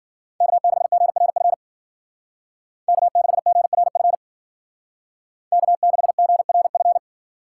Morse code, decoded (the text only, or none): K6GRF K6GRF K6GRF